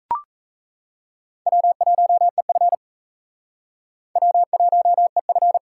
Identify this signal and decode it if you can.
Morse code, heard as a keyed tone, about 38 wpm, 700 Hz. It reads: W1EF W1EF